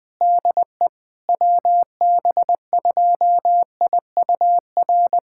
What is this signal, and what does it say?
Morse code, heard as a keyed tone, about 20 wpm, 690 Hz. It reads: DE WB2IUR
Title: SOTA QSO Practice 20wpm